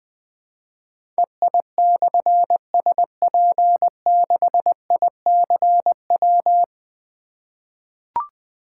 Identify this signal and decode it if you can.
Morse code, heard as a keyed tone, about 20 wpm, 700 Hz. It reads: EI/SP6ICW
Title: Difficult Call Signs 20wpm